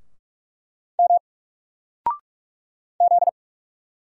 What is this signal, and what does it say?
Morse code, heard as a keyed tone, about 45 wpm, 700 Hz. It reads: M Z
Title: Single Letter-Number 45wpm